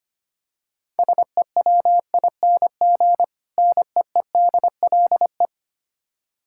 Morse code, heard as a keyed tone, about 25 wpm, 700 Hz. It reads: SEWING NEEDLE
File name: practice / 25wpm / Difficult Words - Speed-Racing with Hints - 25wpm